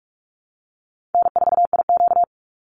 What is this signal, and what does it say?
Morse code, heard as a keyed tone, about 44 wpm, 700 Hz. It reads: N4IQ
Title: Call Signs Repeated 2x - 45wpm